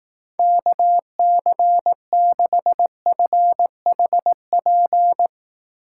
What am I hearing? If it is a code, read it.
Morse code, heard as a keyed tone, about 18 wpm, 700 Hz. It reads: KC6FHP
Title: Speed Racing - USA 2x3 Call Signs 15wpm